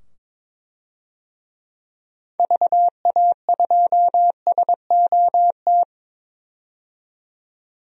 Morse code, heard as a keyed tone, about 22 wpm, 700 Hz. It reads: VA2SOT
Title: DXCC-100 Call Signs - 6 Characters - 22wpm